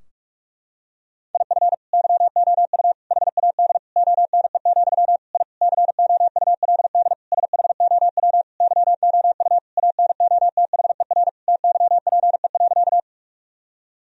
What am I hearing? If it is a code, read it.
Morse code, heard as a keyed tone, about 41 wpm, 700 Hz. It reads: IF YOU HAD ONE, I COULD SHOW YOU ANOTHER TYPE.
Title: Sentences from Top 1000 Words - Review Part 3 - 45wpm